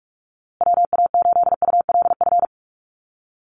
Morse code, heard as a keyed tone, about 44 wpm, 700 Hz. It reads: WA8ULF
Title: USA Call Signs Repeated 2x 45wpm